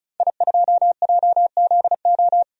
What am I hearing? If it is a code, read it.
Morse code, heard as a keyed tone, about 36 wpm, 700 Hz. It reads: I2JZO
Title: DXCC-100 Call Signs - 5 Characters - 35wpm